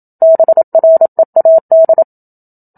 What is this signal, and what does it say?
Morse code, heard as a keyed tone, about 27 wpm, 650 Hz. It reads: BREAD
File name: Book Two / 01-Chapter 1 - 27 WPM